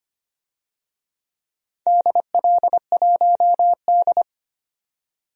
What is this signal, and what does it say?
Morse code, heard as a keyed tone, about 25 wpm, 700 Hz. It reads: DL1D
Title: DXCC-100 Call Signs - 4 Characters - 25wpm